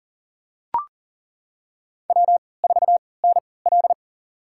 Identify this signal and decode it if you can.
Morse code, heard as a keyed tone, about 40 wpm, spaced as for 23 wpm, 700 Hz. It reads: W4NL
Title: USA Call Signs - 4 Characters - 3x Character Spacing - 40wpm